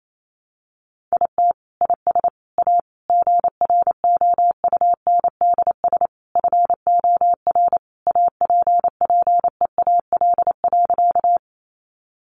Morse code, heard as a keyed tone, about 28 wpm, 700 Hz. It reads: IT IS A GROUNDS FOR APPEAL.
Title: Sentences from Top 1300 Words 28wpm